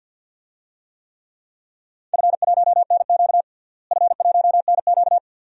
Morse code, heard as a keyed tone, about 51 wpm, 700 Hz. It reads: F1NQ F1NQ
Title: DXCC-100 Call Signs - 4 Characters - Repeated 2x - 50wpm